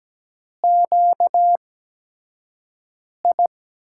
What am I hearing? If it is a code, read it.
Morse code, heard as a keyed tone, about 17 wpm, spaced as for 5 wpm, 700 Hz. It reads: QI
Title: USA Call Signs - 5 Characters - 8x Character Spacing - 17wpm